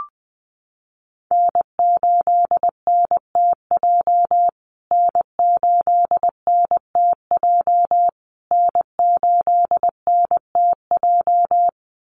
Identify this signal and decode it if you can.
Morse code, heard as a keyed tone, about 20 wpm, 700 Hz. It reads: N8NTJ N8NTJ N8NTJ
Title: USA Call Signs Repeated 3x 20wpm